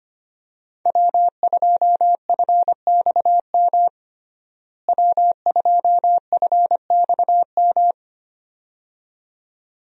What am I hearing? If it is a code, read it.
Morse code, heard as a keyed tone, about 25 wpm, 700 Hz. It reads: W2FXM W2FXM
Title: USA Call Signs - 5 Characters - Repeated 2x - 25wpm